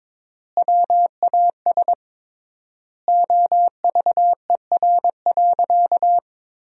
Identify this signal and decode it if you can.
Morse code, heard as a keyed tone, about 22 wpm, 700 Hz. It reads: WAS OVER.
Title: 3x Word-Spacing - Sentences from Top 200 Words - 22wpm